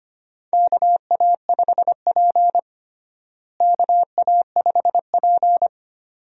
Morse code, heard as a keyed tone, about 25 wpm, 700 Hz. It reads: KA5P KA5P
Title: USA Call Signs - 4 Characters - Repeated 2x - 25wpm